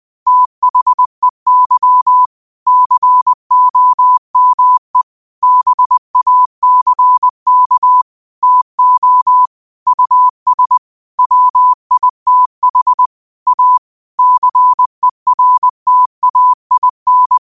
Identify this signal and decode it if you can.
Morse code, heard as a keyed tone, about 20 wpm, 1000 Hz. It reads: THEY COME BACK TO US WITH A CERTAIN